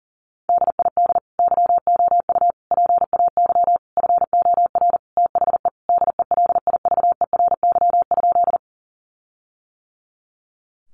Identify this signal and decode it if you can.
Morse code, heard as a keyed tone, about 40 wpm, 700 Hz. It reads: DID YOU PAY FOR THE DELIVERY?
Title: Sentences from Top 1500 Words 40wpm